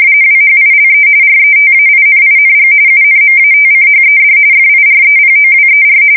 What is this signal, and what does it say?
RTTY, about 45.45 baud. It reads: WHILE RETAINING UNMATCHED FIREPOWER
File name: fsk170 / battleshipnj